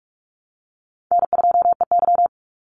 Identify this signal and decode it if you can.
Morse code, heard as a keyed tone, about 44 wpm, 700 Hz. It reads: N2EY
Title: Call Signs Repeated 2x - 45wpm